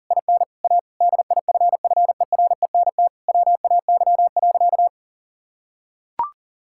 Morse code, heard as a keyed tone, about 40 wpm, 700 Hz. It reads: IN A DIFFERENT WAY.
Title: Sentences from Top 1000 Words - Review Part 6 - 40wpm